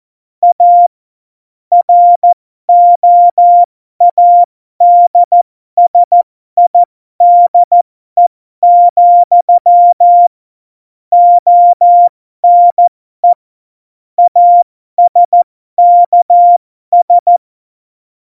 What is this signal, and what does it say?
Morse code, heard as a keyed tone, about 14 wpm, spaced as for 12 wpm, 700 Hz. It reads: A ROADSIDE, ONE ASKS
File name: Morsekurs / Witze